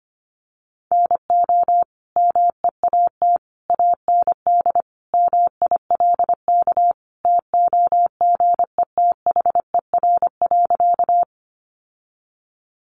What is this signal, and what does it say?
Morse code, heard as a keyed tone, about 25 wpm, 700 Hz. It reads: NO MEAT AND MILK TOGETHER.
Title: Sentences from Top 1500 Words 25wpm